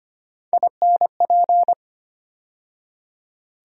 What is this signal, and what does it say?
Morse code, heard as a keyed tone, about 25 wpm, 700 Hz. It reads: INP
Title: USA Call Sign Postfixes - 3 Letters 25wpm